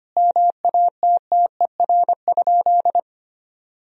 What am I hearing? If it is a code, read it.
Morse code, heard as a keyed tone, about 25 wpm, 700 Hz. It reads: MATTER?
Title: Sentences from Top 1000 Words - Review Part 4 - 25wpm